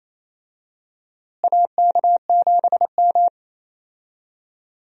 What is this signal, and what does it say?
Morse code, heard as a keyed tone, about 28 wpm, 700 Hz. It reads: AK7M
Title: USA Call Signs - 4 Characters - Repeated 2x - 28wpm